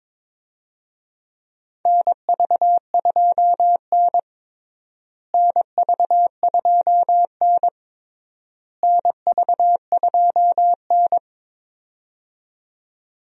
Morse code, heard as a keyed tone, about 22 wpm, 700 Hz. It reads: NV2N NV2N NV2N
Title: USA Call Signs - 4 Characters - Repeated 3x - 22wpm